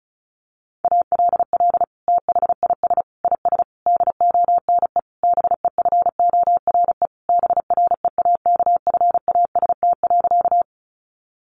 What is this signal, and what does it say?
Morse code, heard as a keyed tone, about 34 wpm, 700 Hz. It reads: ALL THIS IS DONE BEFORE BREAKFAST.
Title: Sentences from Top 1500 Words 35wpm